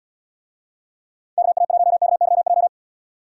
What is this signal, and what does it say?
Morse code, heard as a keyed tone, about 77 wpm, 700 Hz. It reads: KA1KYJ